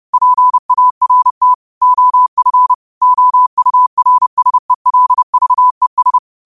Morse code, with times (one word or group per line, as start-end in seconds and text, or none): 0.1-1.5 PART
1.8-2.7 OF
3.0-6.2 OURSELVES